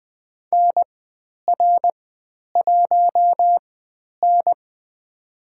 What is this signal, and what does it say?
Morse code, heard as a keyed tone, about 20 wpm, spaced as for 10 wpm, 700 Hz. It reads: NR1N
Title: 102 Lesson 17g - Letter u - Call Signs 20wpm